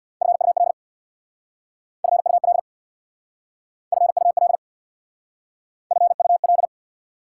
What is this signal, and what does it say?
Morse code, heard as a keyed tone, about 59 wpm, 690 Hz. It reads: FUL FUL FUL FUL
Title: Speed-Racing Prefixes and Suffixes 50wpm